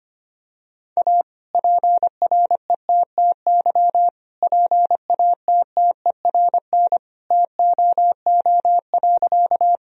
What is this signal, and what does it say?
Morse code, heard as a keyed tone, about 25 wpm, 700 Hz. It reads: A PRETTY PATTERN TOO.
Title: Sentences from Top 1000 Words - Review Part 2 - 25wpm